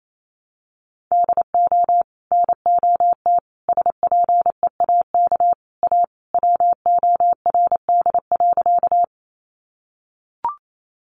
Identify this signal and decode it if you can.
Morse code, heard as a keyed tone, about 28 wpm, 700 Hz. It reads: DO NOT SPEAK A WORD.
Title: Sentences from Top 1100 Words 28wpm